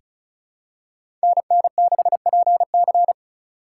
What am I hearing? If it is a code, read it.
Morse code, heard as a keyed tone, about 34 wpm, 700 Hz. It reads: NN6PC
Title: USA Call Signs - 5 Characters - 4x Character Spacing - 35wpm